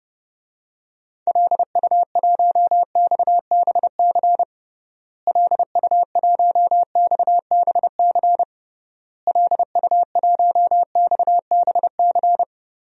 Morse code, heard as a keyed tone, about 32 wpm, 700 Hz. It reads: LU1XBC LU1XBC LU1XBC
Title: DXCC-100 Call Signs - 6 Characters - Repeated 3x - 30wpm